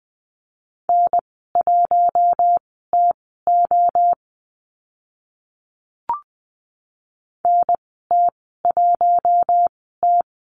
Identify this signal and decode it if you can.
Morse code, heard as a keyed tone, about 20 wpm, spaced as for 15 wpm, 700 Hz. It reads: N1TO NT1T
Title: 048 Lesson 9g - Number 4 - Call Signs 20wpm